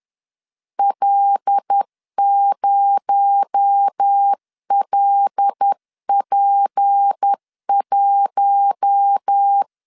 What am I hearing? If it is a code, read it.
Morse code, heard as a keyed tone, about 10 wpm, 800 Hz. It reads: L0LP1
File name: CW / DET / 1 6 1500C S C 10wpm 4252